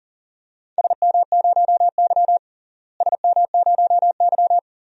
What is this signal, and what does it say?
Morse code, heard as a keyed tone, about 39 wpm, 700 Hz. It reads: SM0Y SM0Y
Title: DXCC-100 Call Signs - 4 Characters - Repeated 2x - 40wpm